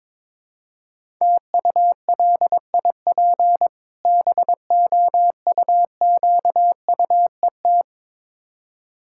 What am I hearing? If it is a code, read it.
Morse code, heard as a keyed tone, about 22 wpm, 690 Hz. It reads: TULIP BOUQUET
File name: Difficult Words with Hints - 22wpm